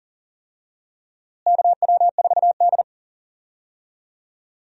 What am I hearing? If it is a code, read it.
Morse code, heard as a keyed tone, about 40 wpm, 700 Hz. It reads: KW4D
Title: USA Call Signs - 4 Characters - Repeated 3x - 40wpm